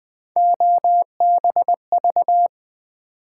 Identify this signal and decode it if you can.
Morse code, heard as a keyed tone, about 20 wpm, 700 Hz. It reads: OBV
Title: Build A Word - 3 Letters At A Time - 9-Letter Words - 20wpm